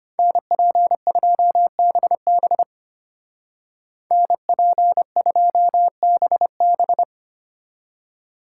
Morse code, heard as a keyed tone, about 33 wpm, 700 Hz. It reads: NP2BB NP2BB